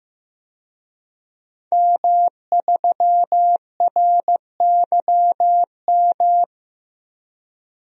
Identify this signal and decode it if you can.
Morse code, heard as a keyed tone, about 15 wpm, 700 Hz. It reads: M3RYM
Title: DXCC-100 Call Signs - 5 Characters - Repeated 2x - 15wpm